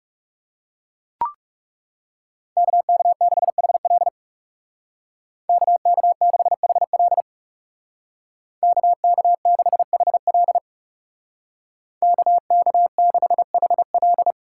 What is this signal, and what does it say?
Morse code, heard as a keyed tone, about 36 wpm, 700 Hz. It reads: KK6HL KK6HL KK6HL KK6HL
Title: Speed Racing - USA 2x2 Call Signs 30wpm